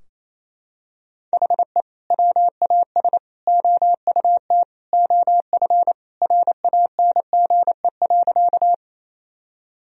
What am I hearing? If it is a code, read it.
Morse code, heard as a keyed tone, about 28 wpm, 700 Hz. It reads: HE WAS OUT OF RANGE.
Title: Sentences from Top 1000 Words - Review Part 4 - 28wpm